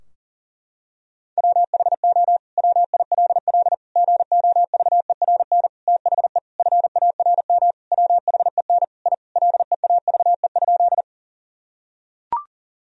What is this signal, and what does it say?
Morse code, heard as a keyed tone, about 40 wpm, 700 Hz. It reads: WHO WILL GOVERN THE FARM WHEN I LEAVE?
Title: Sentences from Top 1000 Words - Review Part 2 - 40wpm